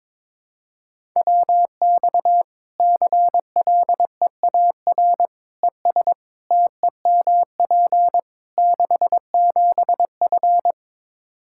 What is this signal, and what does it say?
Morse code, heard as a keyed tone, about 22 wpm, 700 Hz. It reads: WX CLEAR ES TEMP 67F
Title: K1USN SST QSO Practice 22wpm